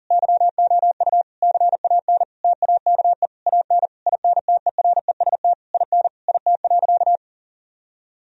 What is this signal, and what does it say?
Morse code, heard as a keyed tone, about 40 wpm, 700 Hz. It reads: YOU CAN TAKE AN INTEREST IN IT.
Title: Sentences from Top 1000 Words - Review Part 2 - 40wpm